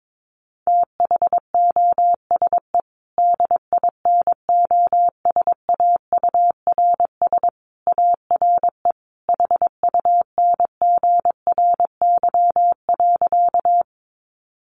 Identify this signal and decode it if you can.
Morse code, heard as a keyed tone, about 22 wpm, 700 Hz. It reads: THOSE DINOSAURS ARE HUNGRY.
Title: A Sound Of Thunder 22wpm